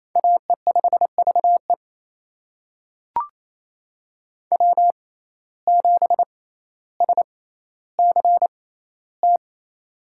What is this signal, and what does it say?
Morse code, heard as a keyed tone, about 28 wpm, spaced as for 14 wpm, 700 Hz. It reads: AE5VE W7SCT
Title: USA Call Signs - 5 Characters - 6x Character Spacing - 28wpm